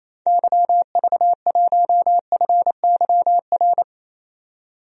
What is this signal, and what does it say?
Morse code, heard as a keyed tone, about 28 wpm, 700 Hz. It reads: YV1FYR
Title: DXCC-100 Call Signs - 6 Characters - 28wpm